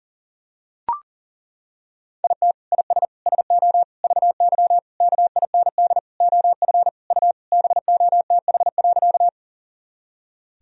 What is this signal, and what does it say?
Morse code, heard as a keyed tone, about 39 wpm, 700 Hz. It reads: IT IS SO VY KIND OF U BOTH.
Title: Sentences from Top 300 Words and Abbreviations 40wpm